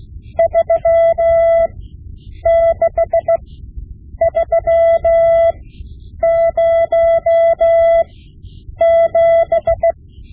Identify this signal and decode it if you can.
Morse code, heard as a keyed tone, about 16 wpm, spaced as for 8 wpm, 670 Hz. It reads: 36307
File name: Morse Lessons / Morse Lessons / Tape-5B